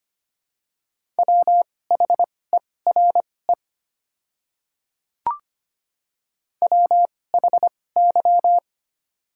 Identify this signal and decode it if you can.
Morse code, heard as a keyed tone, about 25 wpm, spaced as for 18 wpm, 700 Hz. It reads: WHERE WHY
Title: 2x Character Spacing - Top 300 Words - 25wpm